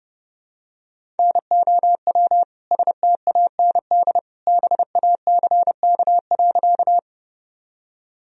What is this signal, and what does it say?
Morse code, heard as a keyed tone, about 30 wpm, 700 Hz. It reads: NOW STAND BACK.
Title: Sentences from Top 1000 Words - Review Part 1 - 30wpm